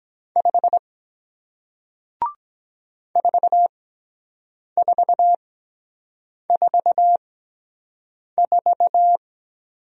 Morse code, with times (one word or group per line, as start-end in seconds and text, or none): 0.4-0.8 5
3.2-9.2 4444